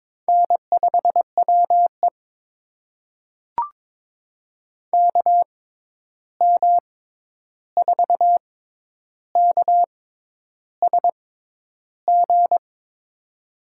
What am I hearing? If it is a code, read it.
Morse code, heard as a keyed tone, about 22 wpm, spaced as for 10 wpm, 700 Hz. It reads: N5WE KM4KSG
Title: USA Call Signs - 6x Character Spacing - 22wpm